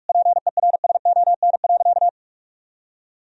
Morse code, heard as a keyed tone, about 46 wpm, 700 Hz. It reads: PERSON.
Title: Sentences from Top 1000 Words - Review Part 3 - 45wpm